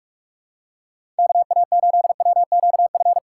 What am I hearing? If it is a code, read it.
Morse code, heard as a keyed tone, about 45 wpm, 700 Hz. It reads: KA8WQF
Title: USA Call Signs - 6 Characters - 4x Character Spacing - 45wpm